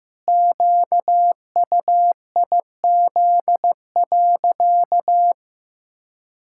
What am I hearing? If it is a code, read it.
Morse code, heard as a keyed tone, about 15 wpm, 700 Hz. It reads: QUIZ.
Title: Pangram Challenge v2 - 15wpm